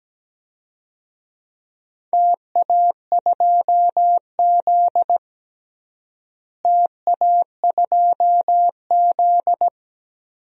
Morse code, heard as a keyed tone, about 17 wpm, 700 Hz. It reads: TA2Z TA2Z
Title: DXCC-100 Call Signs - 4 Characters - Repeated 3x - 17wpm